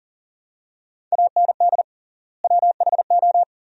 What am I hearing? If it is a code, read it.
Morse code, heard as a keyed tone, about 40 wpm, 700 Hz. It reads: AND WHO
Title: 3x Word-Spacing - Sentences from Top 100 Words - 40wpm